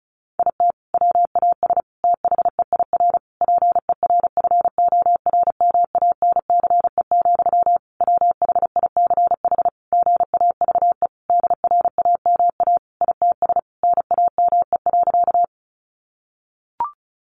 Morse code, heard as a keyed tone, about 34 wpm, 700 Hz. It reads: IT WAS THEIR PERFORMANCE, WHICH GAVE DRAMA ITS NAME.
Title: Sentences from Top 1500 Words 35wpm